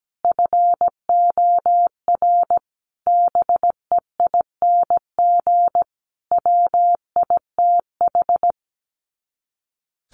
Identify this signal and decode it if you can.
Morse code, heard as a keyed tone, about 17 wpm, 700 Hz. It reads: FOR BEING WITH
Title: Sets of 5 Words Encore 17wpm